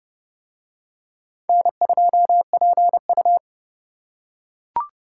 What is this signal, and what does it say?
Morse code, heard as a keyed tone, about 27 wpm, 700 Hz. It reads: N2PU